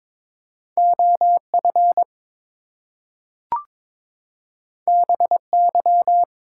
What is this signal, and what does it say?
Morse code, heard as a keyed tone, about 22 wpm, 700 Hz. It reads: OF BY
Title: Two Letter Words - 22wpm